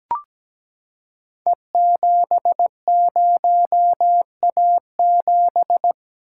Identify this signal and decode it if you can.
Morse code, heard as a keyed tone, about 17 wpm, 700 Hz. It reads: E70A7
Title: Code Groups - 5 Letter:Numbers - 17wpm